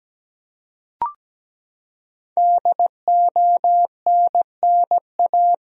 Morse code, heard as a keyed tone, about 17 wpm, 700 Hz. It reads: DONNA
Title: USA Common Names - 17wpm